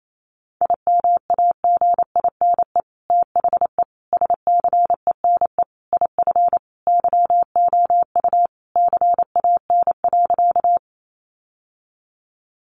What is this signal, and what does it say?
Morse code, heard as a keyed tone, about 28 wpm, 700 Hz. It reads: IMAGINE THE SCENE IF YOU CAN.
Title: Sentences from Top 1400 Words 28wpm